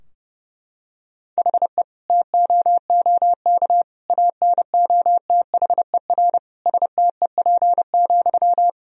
Morse code, heard as a keyed tone, about 32 wpm, 700 Hz. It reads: HE TOOK ANOTHER STEP,
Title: Sentences from Top 1000 Words - Review Part 2 - 30wpm